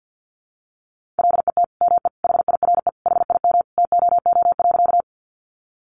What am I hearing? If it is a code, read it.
Morse code, heard as a keyed tone, about 50 wpm, 700 Hz. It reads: LET ME HIRE HIM TOO.
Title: Sentences from Top 1400 Words 50wpm